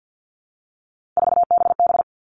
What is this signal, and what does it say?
Morse code, heard as a keyed tone, about 50 wpm, 700 Hz. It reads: 4BB
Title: All Cut Numbers - 3-Digit 50wpm